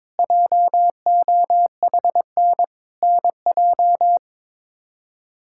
Morse code, heard as a keyed tone, about 21 wpm, 700 Hz. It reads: JOHN NJ
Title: K1USN SST QSO Practice - 22wpm